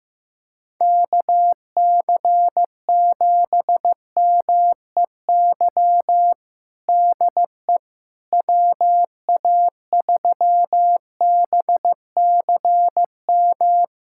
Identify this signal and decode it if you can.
Morse code, heard as a keyed tone, about 16 wpm, 700 Hz. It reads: KC7MEY DE WA3BCM
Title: USA Callsign DE Callsign 15wpm